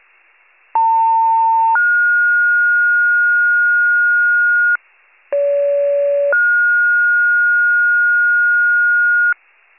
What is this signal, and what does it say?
Two-tone page, heard as a one-second tone, then a three-second tone, about 900 then 1430 Hz, about 570 then 1430 Hz. It is heard from 0.7 to 9.3 s.